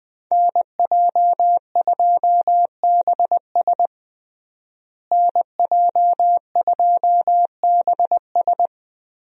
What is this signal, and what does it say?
Morse code, heard as a keyed tone, about 21 wpm, 700 Hz. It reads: NJ2BS NJ2BS
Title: USA Call Signs - 5 Characters - Repeated 3x - 20wpm